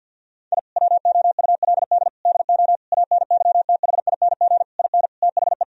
Morse code, heard as a keyed tone, about 50 wpm, 700 Hz. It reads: I WOULD DO ANYTHING IN THE
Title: Sentences from Top 1000 Words - Review Part 2 - 50wpm